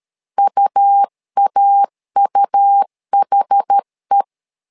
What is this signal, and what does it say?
Morse code, heard as a keyed tone, about 12 wpm, 800 Hz. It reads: UAUHE
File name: LEZIONI / 2 UHDN / 2 UH 12wpm 1439